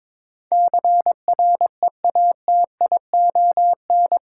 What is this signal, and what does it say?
Morse code, heard as a keyed tone, about 22 wpm, 700 Hz. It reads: CREATION
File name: practice / 22wpm / Eight Letter Words - 22wpm